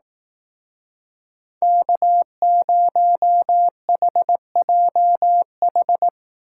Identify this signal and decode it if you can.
Morse code, heard as a keyed tone, about 18 wpm, 700 Hz. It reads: K0HJH
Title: Speed Racing - USA 1x3 Call Signs 15wpm